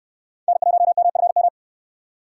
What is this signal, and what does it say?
Morse code, heard as a keyed tone, about 68 wpm, 700 Hz. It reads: N1MFM